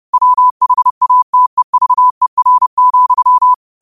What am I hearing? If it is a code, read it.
Morse code, heard as a keyed tone, about 30 wpm, 1000 Hz. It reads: WHATEVER,
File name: qotdfiles / 30wpm